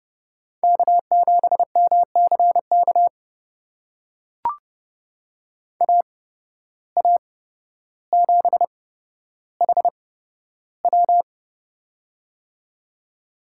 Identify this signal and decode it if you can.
Morse code, heard as a keyed tone, about 31 wpm, spaced as for 13 wpm, 700 Hz. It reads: K7MCK AA7HW
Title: USA Call Signs - 5 Characters - 8x Character Spacing - 30wpm